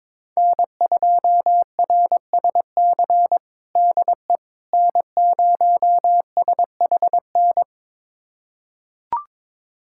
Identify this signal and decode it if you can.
Morse code, heard as a keyed tone, about 21 wpm, 700 Hz. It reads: N2RSC DE N0SHN
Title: USA Callsign DE Callsign - 5 Characters - 22wpm